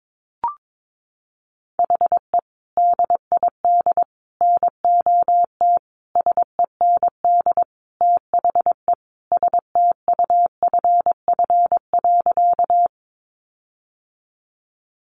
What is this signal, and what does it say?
Morse code, heard as a keyed tone, about 22 wpm, 700 Hz. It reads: HE DID NOT SEND THE STUFF.
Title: Sentences from Top 1500 Words 22wpm